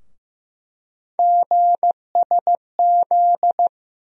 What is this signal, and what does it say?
Morse code, heard as a keyed tone, about 15 wpm, 700 Hz. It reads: GSZ
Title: USA Call Sign Postfixes - 3 Letters 15wpm